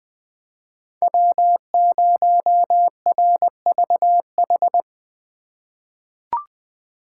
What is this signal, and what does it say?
Morse code, heard as a keyed tone, about 20 wpm, 700 Hz. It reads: W0RVH
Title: USA Call Signs - 5 Characters - 2x Character Spacing - 20wpm